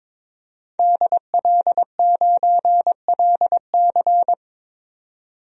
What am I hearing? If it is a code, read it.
Morse code, heard as a keyed tone, about 22 wpm, 700 Hz. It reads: DL9LC
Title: DXCC-100 Call Signs - 5 Characters - 22wpm